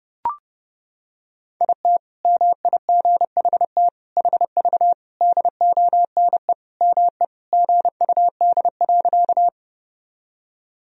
Morse code, heard as a keyed tone, about 30 wpm, 700 Hz. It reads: IT MIGHT HV DONE ME GUD.
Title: Sentences from Top 500 Words and Abbreviations 30wpm